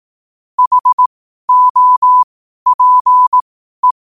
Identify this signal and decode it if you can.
Morse code, heard as a keyed tone, about 18 wpm, spaced as for 12 wpm, 1000 Hz. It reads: HOPE
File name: qotdfiles / 12wpm